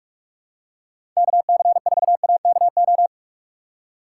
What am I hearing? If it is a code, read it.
Morse code, heard as a keyed tone, about 46 wpm, 700 Hz. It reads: KC4AKO
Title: USA Call Signs - 6 Characters - 2x Character Spacing - 45wpm